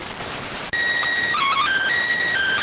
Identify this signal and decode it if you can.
SSTV (slow-scan television), PD 290.